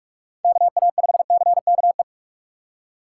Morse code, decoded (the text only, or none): KA5CKE